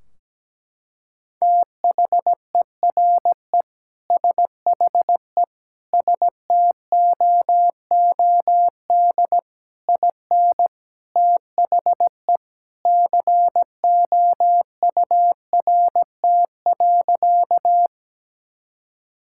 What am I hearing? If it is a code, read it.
Morse code, heard as a keyed tone, about 17 wpm, 700 Hz. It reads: THERE SHE STOOD IN THE COURT.
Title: Sentences from Top 1600 Words 17wpm